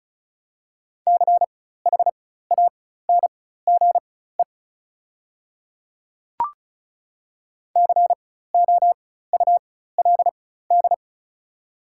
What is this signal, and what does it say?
Morse code, heard as a keyed tone, about 34 wpm, spaced as for 16 wpm, 700 Hz. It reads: CHANGE COULD
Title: 4x Character Spacing - Top 200 Words - 35wpm